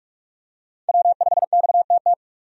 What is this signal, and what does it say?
Morse code, heard as a keyed tone, about 46 wpm, 700 Hz. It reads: W5XTT
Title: USA Call Signs - 5 Characters - 45wpm